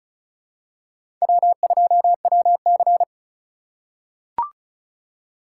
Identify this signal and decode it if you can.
Morse code, heard as a keyed tone, about 34 wpm, 700 Hz. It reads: W2WC
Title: USA Call Signs - 4 Characters - 2x Character Spacing - 35wpm